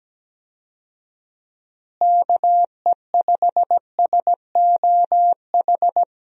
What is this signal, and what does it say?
Morse code, heard as a keyed tone, about 17 wpm, 700 Hz. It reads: KE5SOH
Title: USA Call Signs - 6 Characters - Repeated 3x - 17wpm